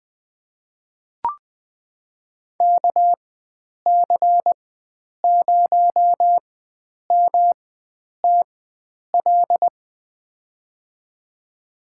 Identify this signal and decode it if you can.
Morse code, heard as a keyed tone, about 20 wpm, spaced as for 9 wpm, 700 Hz. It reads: KC0MTL
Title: USA Call Signs - 6 Characters - 4x Character Spacing - 20wpm